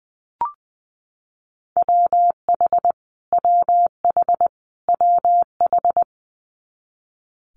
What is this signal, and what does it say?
Morse code, heard as a keyed tone, about 19 wpm, 700 Hz. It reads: WH WH WH
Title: Multi-letter Phonemes Repeated 3x 20wpm